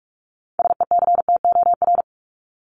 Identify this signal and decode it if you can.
Morse code, heard as a keyed tone, about 44 wpm, 700 Hz. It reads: SECTOR
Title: Top 1751-2000 Words 45wpm